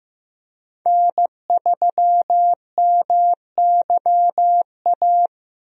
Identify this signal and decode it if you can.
Morse code, heard as a keyed tone, about 15 wpm, 700 Hz. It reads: N3MYA
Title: USA Call Signs - 5 Characters - Repeated 2x - 15wpm